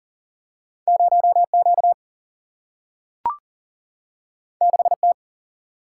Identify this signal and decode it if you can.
Morse code, heard as a keyed tone, about 40 wpm, 700 Hz. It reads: 0Q 6T
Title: Single-Letter Single-Number 40wpm